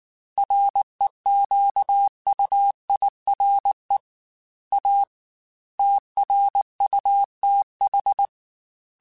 Morse code, heard as a keyed tone, about 19 wpm, 800 Hz. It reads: REQUIRE A TRUTH
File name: Technical Subjects / Technical Subjects-mp3s / AND Gate @18